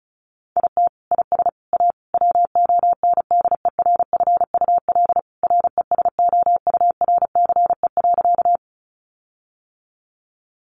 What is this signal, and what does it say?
Morse code, heard as a keyed tone, about 34 wpm, 700 Hz. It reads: IT IS A WONDERFUL RESOURCE.
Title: Sentences from Top 1300 Words 35wpm